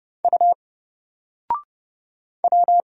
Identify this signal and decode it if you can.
Morse code, heard as a keyed tone, about 31 wpm, 700 Hz. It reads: U W
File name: Single-Letter-Number-30wpm